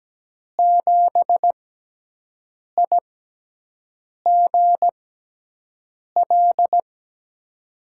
Morse code, heard as a keyed tone, about 17 wpm, spaced as for 6 wpm, 700 Hz. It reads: 7IGL
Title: USA Call Signs - 6 Characters - 6x Character Spacing - 17wpm